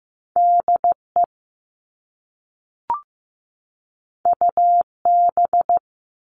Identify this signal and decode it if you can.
Morse code, heard as a keyed tone, about 15 wpm, 700 Hz. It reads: DE UB
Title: All Cut Numbers - 2-Digit 15wpm